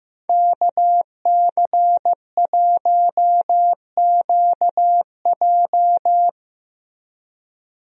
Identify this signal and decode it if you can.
Morse code, heard as a keyed tone, about 16 wpm, 700 Hz. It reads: KC1QJ
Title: USA Call Signs - 5 Characters - Repeated 2x - 15wpm